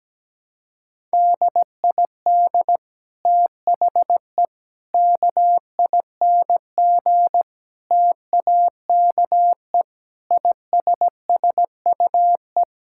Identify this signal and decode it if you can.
Morse code, heard as a keyed tone, about 17 wpm, 700 Hz. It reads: DID THE KING TAKE ISSUE
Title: Sentences from Top 1000 Words - Review Part 6 - 17wpm